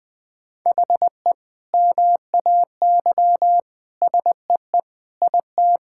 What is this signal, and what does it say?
Morse code, heard as a keyed tone, about 21 wpm, 700 Hz. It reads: HE MAY SEE IT
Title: Sentences from Top 1600 Words 20wpm